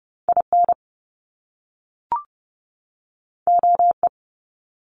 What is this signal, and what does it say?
Morse code, heard as a keyed tone, about 30 wpm, 700 Hz. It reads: IN OE
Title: Top Two Letter Combinations in Words 30wpm